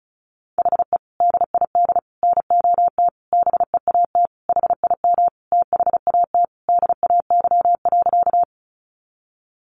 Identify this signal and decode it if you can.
Morse code, heard as a keyed tone, about 34 wpm, 700 Hz. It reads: HE DID NOT BEAT HIM THAT DAY.
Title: Sentences from Top 1300 Words 35wpm